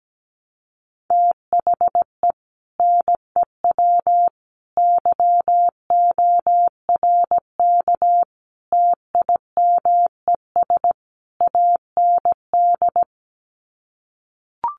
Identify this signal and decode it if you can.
Morse code, heard as a keyed tone, about 17 wpm, 700 Hz. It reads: THE NEW YORK TIMES AND
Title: Sets of 5 Words Encore 17wpm